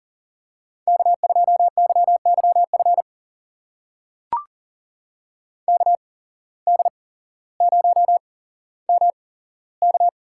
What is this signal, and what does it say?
Morse code, heard as a keyed tone, about 39 wpm, spaced as for 17 wpm, 700 Hz. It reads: K2YYF KD0MK
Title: USA Call Signs - 8x Character Spacing - 40wpm